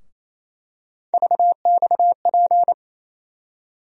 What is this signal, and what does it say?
Morse code, heard as a keyed tone, about 27 wpm, 700 Hz. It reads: VXP
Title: USA Call Sign Postfixes - 3 Letters 28wpm